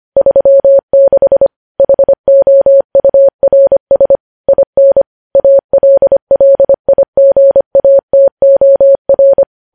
Morse code, heard as a keyed tone, about 25 wpm, 550 Hz. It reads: 36 HOURS IN ALLIGATOR
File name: podcast / News headlines-1x-25x25